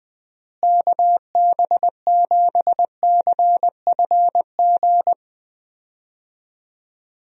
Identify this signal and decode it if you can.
Morse code, heard as a keyed tone, about 20 wpm, 700 Hz. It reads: KB7CFG
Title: USA Call Signs - 6 Characters - Repeated 3x - 20wpm